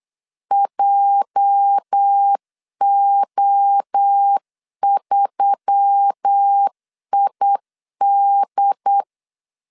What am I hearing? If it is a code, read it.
Morse code, heard as a keyed tone, about 8 wpm, 800 Hz. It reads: JO3ID
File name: LEZIONI / 9 286 8wpm 3425